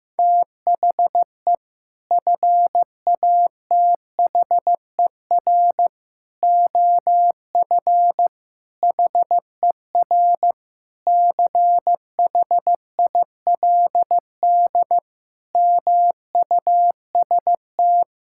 Morse code, heard as a keyed tone, about 15 wpm, 700 Hz. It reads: THE FATHER OF HER CHILD MUST
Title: Sentences from Top 1000 Words - Review Part 5 - 15wpm